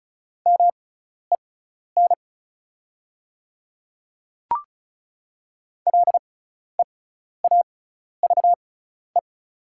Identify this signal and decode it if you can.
Morse code, heard as a keyed tone, about 36 wpm, spaced as for 12 wpm, 690 Hz. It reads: MEN LEAVE
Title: 6x Character Spacing - Top 500 Words - 35wpm